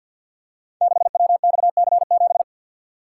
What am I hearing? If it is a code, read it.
Morse code, heard as a keyed tone, about 53 wpm, 700 Hz. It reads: 6WXC7
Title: Code Groups - 5 Letter:Numbers - 50wpm